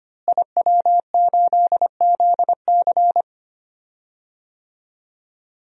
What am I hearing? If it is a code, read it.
Morse code, heard as a keyed tone, about 25 wpm, 700 Hz. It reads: IW8ZC
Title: DXCC-100 Call Signs - 5 Characters - Repeated 3x - 25wpm